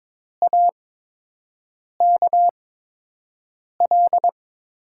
Morse code, heard as a keyed tone, about 22 wpm, 700 Hz. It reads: A K L
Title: USA Call Signs - 5 Characters - 8x Character Spacing - 22wpm